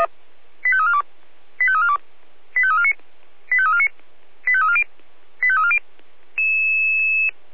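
Five-tone selective call (ZVEI), heard as quick runs of five tones, tones of about 80 ms. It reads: 85342 85342 85349 85349 85340 85340
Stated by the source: ZVEI five-tone selective call received by SDR.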